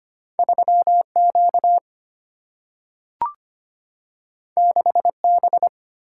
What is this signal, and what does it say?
Morse code, heard as a keyed tone, about 25 wpm, 700 Hz. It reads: 3 Q 6 B
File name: Two Letter-Number Combinations - 25wpm